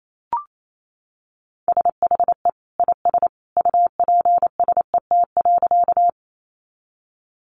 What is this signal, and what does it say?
Morse code, heard as a keyed tone, about 28 wpm, 700 Hz. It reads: SHE IS UPSET.
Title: Sentences from Top 800 Words 28wpm